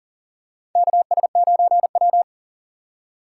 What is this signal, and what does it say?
Morse code, heard as a keyed tone, about 40 wpm, 700 Hz. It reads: KS9W
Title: K1USN SST QSO Practice - 40wpm